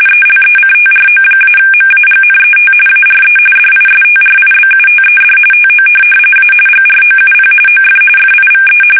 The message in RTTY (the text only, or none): ELECTRO-MAGNET ARE DISTRIBUTED TO A GROUP OF FIVE